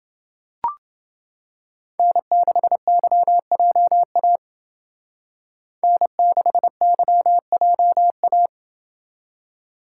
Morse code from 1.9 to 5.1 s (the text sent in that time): N6YJA